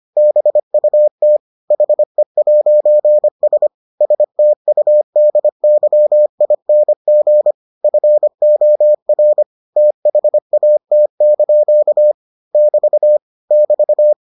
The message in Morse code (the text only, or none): BUT HE'S STUDYING FOR THAT) <BT> <BT>